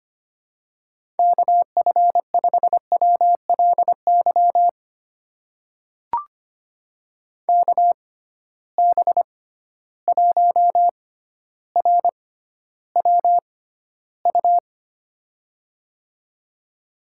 Morse code, heard as a keyed tone, about 25 wpm, spaced as for 13 wpm, 700 Hz. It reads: KF5WLY KB1RWU
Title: USA Call Signs - 6 Characters - 6x Character Spacing - 25wpm